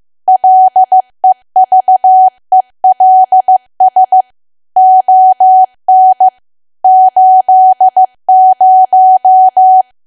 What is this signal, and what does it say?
Morse code, heard as a keyed tone, about 15 wpm, 750 Hz. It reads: LEVELS ON 80
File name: cwa / W1AW-Level2 / 15WPM